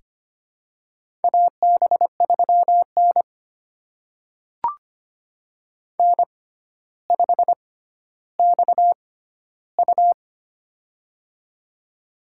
Morse code, heard as a keyed tone, about 25 wpm, spaced as for 13 wpm, 700 Hz. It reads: AB3N N5XU
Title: USA Call Signs - 4 Characters - 6x Character Spacing - 25wpm